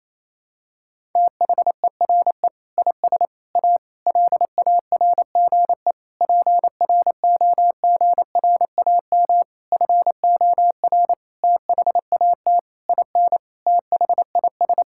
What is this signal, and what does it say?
Morse code, heard as a keyed tone, about 28 wpm, 700 Hz. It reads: THERE IS A LARGE PROGRAM FOR THAT IN THIS
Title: Sentences from Top 1000 Words - Review Part 5 - 28wpm